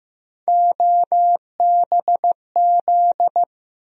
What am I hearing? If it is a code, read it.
Morse code, heard as a keyed tone, about 16 wpm, 700 Hz. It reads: OBZ
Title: USA Call Sign Postfixes - 3 Letters 15wpm